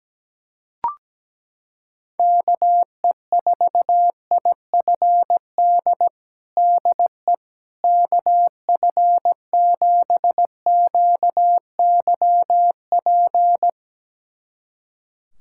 Morse code, heard as a keyed tone, about 17 wpm, 700 Hz. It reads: KE4IFD DE KF7QYP
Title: USA Callsign DE Callsign - 6 Characters - 17wpm